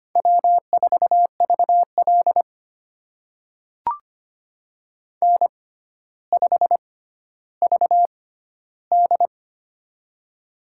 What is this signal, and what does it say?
Morse code, heard as a keyed tone, about 25 wpm, spaced as for 13 wpm, 700 Hz. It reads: W4VL N5VD
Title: USA Call Signs - 4 Characters - 6x Character Spacing - 25wpm